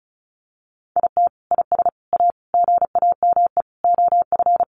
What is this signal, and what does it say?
Morse code, heard as a keyed tone, about 34 wpm, 700 Hz. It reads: IT IS A GAME OF
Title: Sentences from Top 1200 Words 35wpm